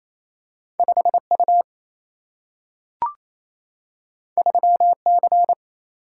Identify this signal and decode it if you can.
Morse code, heard as a keyed tone, about 28 wpm, 700 Hz. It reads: 5 U 3 C
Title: Single-Letter Single-Number 28wpm